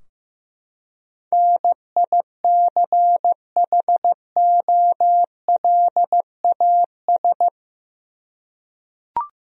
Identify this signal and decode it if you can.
Morse code, heard as a keyed tone, about 15 wpm, 700 Hz. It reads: NICHOLAS
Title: USA Common Names - 15wpm